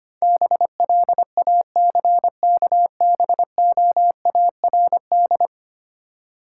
Morse code, heard as a keyed tone, about 26 wpm, 700 Hz. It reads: BLACKBOARD
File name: Easy Words - Speed-Racing with Hints - 25wpm